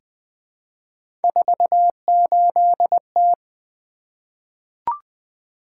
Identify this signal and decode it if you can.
Morse code, heard as a keyed tone, about 20 wpm, 700 Hz. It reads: 48T
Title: Three Letter-Number Combinations - 20wpm